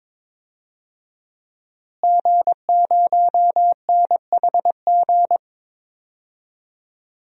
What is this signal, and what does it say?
Morse code, heard as a keyed tone, about 22 wpm, 700 Hz. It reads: G0NHG
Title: DXCC-100 Call Signs - 5 Characters - Repeated 2x - 22wpm